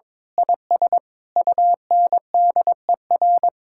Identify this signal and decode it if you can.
Morse code, heard as a keyed tone, about 22 wpm, 700 Hz. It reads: IS UNDER
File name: Sentences with 9-Word Phrases - 22wpm